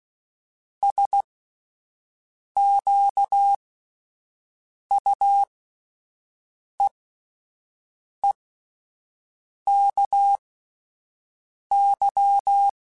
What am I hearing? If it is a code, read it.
Morse code, heard as a keyed tone, about 16 wpm, spaced as for 5 wpm, 800 Hz. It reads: SQUEEKY
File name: Spaced / Chuckles Spaced-mp3s / Squeaky sp @10